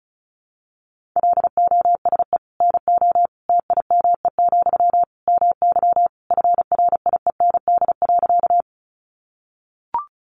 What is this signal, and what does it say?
Morse code, heard as a keyed tone, about 34 wpm, 700 Hz. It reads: LOSE NO TIME, MY FRIEND.
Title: Sentences from Top 1100 Words 35wpm